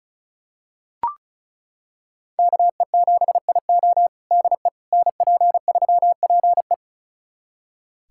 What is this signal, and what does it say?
Morse code, heard as a keyed tone, about 36 wpm, 700 Hz. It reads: KE7IO DE NP3PE
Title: USA Callsign DE Callsign - 5 Characters - 35wpm